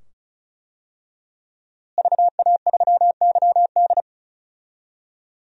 Morse code, heard as a keyed tone, about 36 wpm, 700 Hz. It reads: VA3YD
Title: DXCC-100 Call Signs - 5 Characters - Repeated 2x - 35wpm